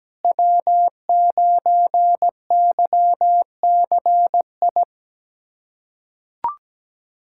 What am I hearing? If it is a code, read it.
Morse code, heard as a keyed tone, about 17 wpm, 700 Hz. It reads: W9YCI
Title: USA Callsign DE Callsign - 5 Characters - 17wpm